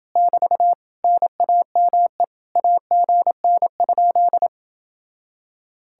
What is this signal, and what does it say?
Morse code, heard as a keyed tone, about 28 wpm, 700 Hz. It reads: <BT> NAME AGN?